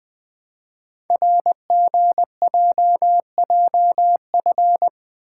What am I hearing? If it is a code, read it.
Morse code, heard as a keyed tone, about 20 wpm, 700 Hz. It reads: RGJJF
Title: Code Groups - 5 Letters - 20wpm